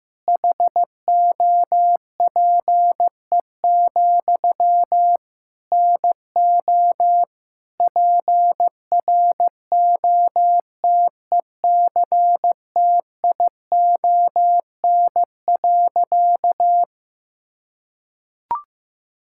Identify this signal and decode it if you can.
Morse code, heard as a keyed tone, about 16 wpm, 700 Hz. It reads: HOPE, NO PROTECTION.
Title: Sentences from Top 1600 Words 15wpm